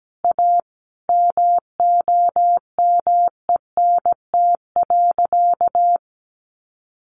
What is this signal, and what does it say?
Morse code, heard as a keyed tone, about 17 wpm, 700 Hz. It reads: A MOMENT.
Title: Sentences from Top 1000 Words 17wpm